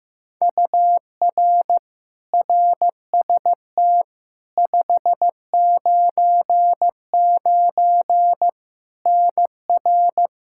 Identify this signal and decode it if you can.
Morse code, heard as a keyed tone, about 15 wpm, 700 Hz. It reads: UR RST 599 NR
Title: K1USN SST QSO Practice 15wpm